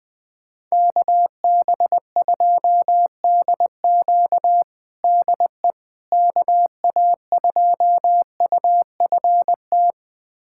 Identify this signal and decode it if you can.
Morse code, heard as a keyed tone, about 20 wpm, 700 Hz. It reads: KB2DQ DE KA2UFT
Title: USA Callsign DE Callsign 20wpm